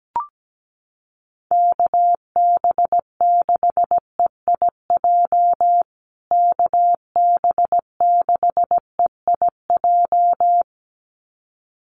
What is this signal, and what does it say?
Morse code, heard as a keyed tone, about 17 wpm, 700 Hz. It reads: KB6EIJ KB6EIJ
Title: USA Call Signs Repeated 2x 17wpm